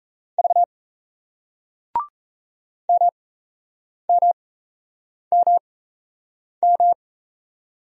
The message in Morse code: V MMMM

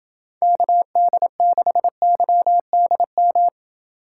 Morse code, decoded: KD6YDM

